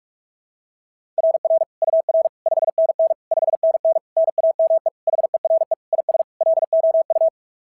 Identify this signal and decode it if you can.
Morse code, heard as a keyed tone, about 44 wpm, 660 Hz. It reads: RR UR 5NN 5NN NAME HERE IS LOU
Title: SOTA QSO Practice 45wpm